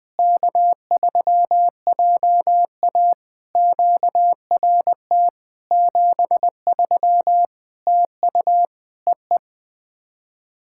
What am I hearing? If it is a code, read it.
Morse code, heard as a keyed tone, about 20 wpm, 700 Hz. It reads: K3JA QRT 73 TU EE
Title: POTA QSO Practice - 20wpm